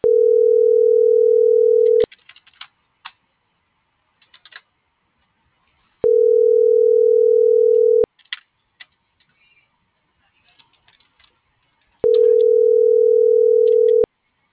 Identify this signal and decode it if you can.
Ringback tone (North America), 440 + 480 Hz.
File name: CENTRAL TERRESTRE / out jessamynvillafuerte